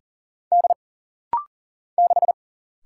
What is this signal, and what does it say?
Morse code, heard as a keyed tone, about 40 wpm, 700 Hz. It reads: D 6
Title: Single Letter-Number Alt - Rapid-Fire 40wpm